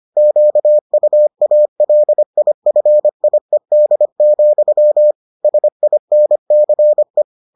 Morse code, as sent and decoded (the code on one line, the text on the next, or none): --.- ..- .- .-.. .. ..-. .. -.. --..-- / ... .. -. -.-. .
QUALIFIED, SINCE